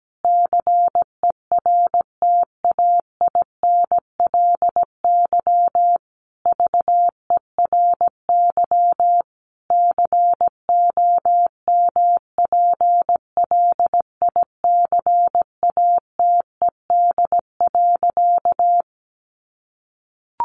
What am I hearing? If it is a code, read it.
Morse code, heard as a keyed tone, about 17 wpm, 700 Hz. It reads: CERTAINLY VERY COMPLICATED.
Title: Sentences from Top 1200 Words 17wpm